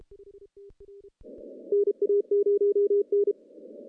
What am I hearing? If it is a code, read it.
Morse code, heard as a keyed tone, about 29 wpm, 410 Hz. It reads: NA0N
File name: contests / CW-SS-k5tr- -0100